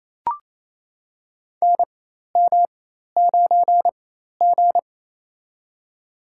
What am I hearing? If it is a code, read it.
Morse code, heard as a keyed tone, about 28 wpm, spaced as for 13 wpm, 700 Hz. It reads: NM9G
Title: USA Call Signs - 4 Characters - 4x Character Spacing - 28wpm